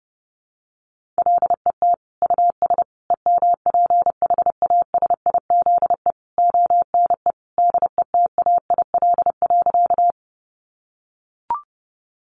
Morse code, heard as a keyed tone, about 30 wpm, 700 Hz. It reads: LET US EMPHASIZE ONE DETAIL.
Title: Sentences from Top 1500 Words 30wpm